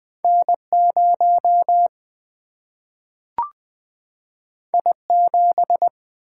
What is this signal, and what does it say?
Morse code, heard as a keyed tone, about 21 wpm, 700 Hz. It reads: N0 I7
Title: Single-Letter Single-Number 20wpm